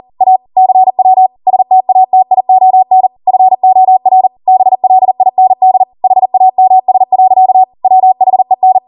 Morse code, heard as a keyed tone, about 40 wpm, 750 Hz. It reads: A CW STATION FOR BLIND HAMS. WHEN